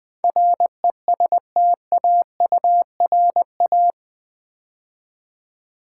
Morse code, heard as a keyed tone, about 20 wpm, 700 Hz. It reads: RESTAURA
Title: Build A Word - 4 Letters At A Time - 10-Letter Words - 20wpm